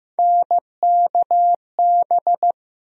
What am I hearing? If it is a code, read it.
Morse code, heard as a keyed tone, about 15 wpm, 700 Hz. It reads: NKB
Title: USA Call Sign Postfixes - 3 Letters 15wpm